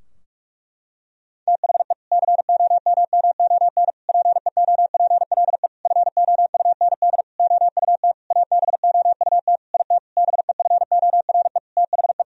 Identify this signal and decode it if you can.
Morse code, heard as a keyed tone, about 41 wpm, 700 Hz. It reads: THE COMMON PEOPLE FOUND OUT ABOUT IT BEFORE THE
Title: Sentences from Top 1000 Words - Review Part 3 - 45wpm